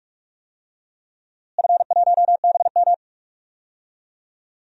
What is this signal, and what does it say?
Morse code, heard as a keyed tone, about 46 wpm, 700 Hz. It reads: F1BM
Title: DXCC-100 Call Signs - 4 Characters - Repeated 2x - 45wpm